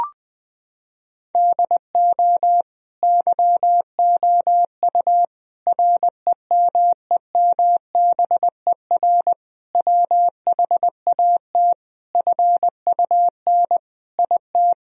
Morse code, heard as a keyed tone, about 20 wpm, 700 Hz. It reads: DO YOU REMEMBER WHAT FUN IT
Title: Sentences from Top 1000 Words - Review Part 4 - 20wpm